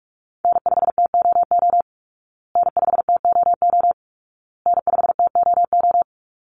Morse code, heard as a keyed tone, about 44 wpm, 700 Hz. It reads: N5TOO N5TOO N5TOO
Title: Call Signs Repeated 2x - 45wpm